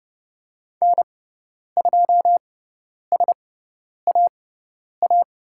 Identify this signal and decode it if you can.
Morse code, heard as a keyed tone, about 30 wpm, spaced as for 10 wpm, 700 Hz. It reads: N2SAA
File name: Lesson 18g - Letter m - Call Signs 30wpm